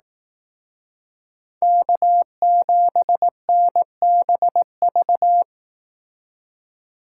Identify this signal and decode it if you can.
Morse code, heard as a keyed tone, about 18 wpm, 700 Hz. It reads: K7NBV
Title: Speed Racing - USA 1x3 Call Signs 15wpm